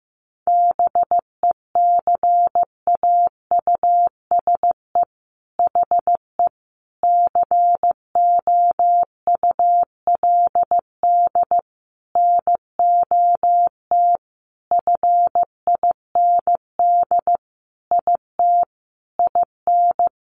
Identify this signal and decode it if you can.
Morse code, heard as a keyed tone, about 15 wpm, 700 Hz. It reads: BECAUSE HE COULD NOT FIND IT IN
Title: Sentences from Top 1400 Words 15wpm